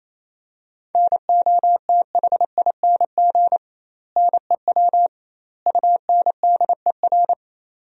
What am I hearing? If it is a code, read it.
Morse code, heard as a keyed tone, about 28 wpm, 700 Hz. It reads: NOTHING NEW UNDER
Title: 2x Word-Spacing - Sentences from Top 500 Words - 28wpm